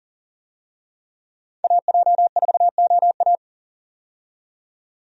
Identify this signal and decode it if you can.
Morse code, heard as a keyed tone, about 40 wpm, 700 Hz. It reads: AJ4OA
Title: USA Call Signs - 5 Characters - 40wpm